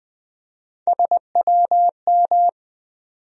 Morse code, heard as a keyed tone, about 21 wpm, 700 Hz. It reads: SWM
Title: USA Call Sign Postfixes - 3 Letters 20wpm